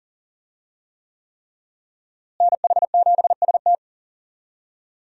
Morse code, heard as a keyed tone, about 40 wpm, 700 Hz. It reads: NH7ST
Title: USA Call Signs - 5 Characters - Repeated 3x - 40wpm